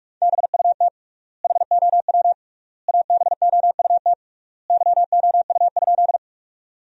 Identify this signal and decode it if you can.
Morse code, heard as a keyed tone, about 46 wpm, 690 Hz. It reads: BUT HOW ABOUT YOU?
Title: 3x Word-Spacing - Sentences from Top 100 Words - 45wpm